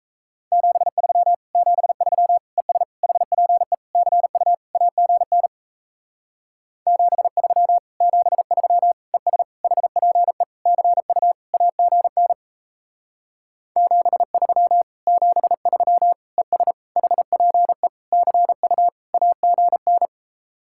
Morse code, heard as a keyed tone, about 37 wpm, 700 Hz. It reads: 73 73 ES HPE CU AGN 73 73 ES HPE CU AGN 73 73 ES HPE CU AGN